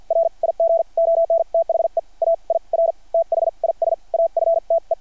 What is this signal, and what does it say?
Morse code, heard as a keyed tone, about 53 wpm, spaced as for 42 wpm, 660 Hz. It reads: RIG ON THE AIR THIS AFTE
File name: archived / CRSnet